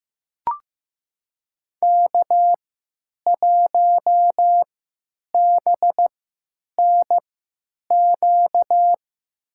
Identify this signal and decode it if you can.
Morse code, heard as a keyed tone, about 15 wpm, spaced as for 9 wpm, 700 Hz. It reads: K1BNQ
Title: USA Call Signs - 5 Characters - 3x Character Spacing - 15wpm